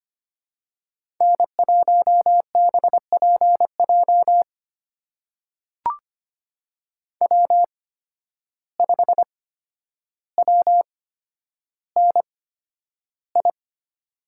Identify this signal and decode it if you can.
Morse code, heard as a keyed tone, about 25 wpm, spaced as for 11 wpm, 700 Hz. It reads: N1BPJ W5WNI
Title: USA Call Signs - 5 Characters - 8x Character Spacing - 25wpm